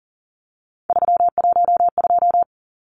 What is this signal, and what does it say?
Morse code, heard as a keyed tone, about 40 wpm, 700 Hz. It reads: 312
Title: Numbers - 3-Digit 40wpm